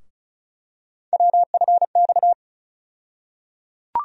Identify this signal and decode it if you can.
Morse code, heard as a keyed tone, about 36 wpm, 700 Hz. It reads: WFX
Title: USA Call Sign Postfixes - 3 Letters 35wpm